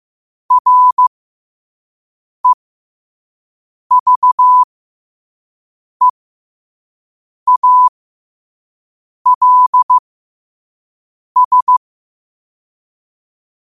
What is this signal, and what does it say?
Morse code, heard as a keyed tone, about 15 wpm, spaced as for 5 wpm, 1000 Hz. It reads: REVEALS
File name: qotdfiles / 5wpm